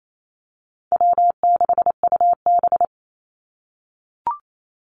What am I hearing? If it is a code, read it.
Morse code, heard as a keyed tone, about 28 wpm, 700 Hz. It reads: W6UB
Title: NA SOTA Activator Callsigns 28wpm